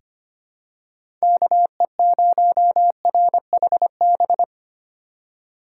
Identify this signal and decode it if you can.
Morse code, heard as a keyed tone, about 25 wpm, 700 Hz. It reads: KE0RHB
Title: USA Call Signs - 6 Characters - 4x Character Spacing - 25wpm